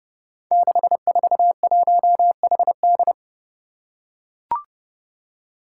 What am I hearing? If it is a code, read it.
Morse code, heard as a keyed tone, about 32 wpm, 700 Hz. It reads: 641HD
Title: Five Challenging Letter-Number Combinations 30wpm